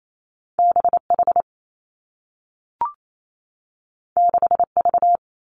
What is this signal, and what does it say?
Morse code, heard as a keyed tone, about 28 wpm, 700 Hz. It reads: B H 6 V
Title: Two Challenging Letter-Number Combinations 28wpm